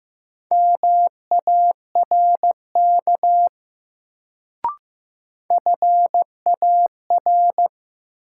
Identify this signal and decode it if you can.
Morse code, heard as a keyed tone, about 15 wpm, 700 Hz. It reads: MARK FAR
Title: Sentences from Top 300 Words - New Word Drill - 15wpm